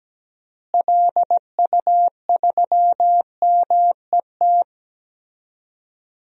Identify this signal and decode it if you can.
Morse code, heard as a keyed tone, about 17 wpm, 700 Hz. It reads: LU3MET
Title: DXCC-100 Call Signs - 6 Characters - Repeated 3x - 17wpm